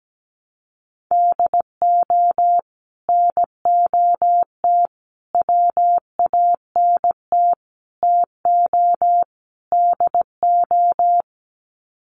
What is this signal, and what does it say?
Morse code, heard as a keyed tone, about 17 wpm, 700 Hz. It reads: DO NOT WANT TO DO
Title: Sets of 5 Words 17wpm